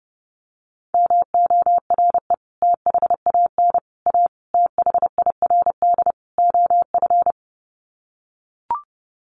Morse code, heard as a keyed tone, about 28 wpm, 700 Hz. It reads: MORE THAN A THIRD OF
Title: Sets of 5 Words Encore2 30wpm